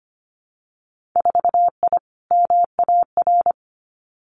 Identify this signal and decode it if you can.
Morse code, heard as a keyed tone, about 25 wpm, 700 Hz. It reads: 4I MAR
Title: Winter Field Day Exchanges Encore 25wpm